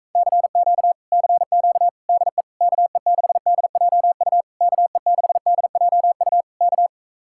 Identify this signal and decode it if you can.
Morse code, heard as a keyed tone, about 44 wpm, 700 Hz. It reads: CQ CQ DE KE6DJU KE6DJU K